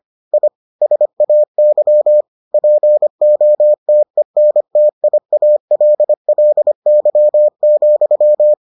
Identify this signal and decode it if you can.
Morse code, heard as a keyed tone, about 25 wpm, 600 Hz. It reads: I SAY POTENTIALLY,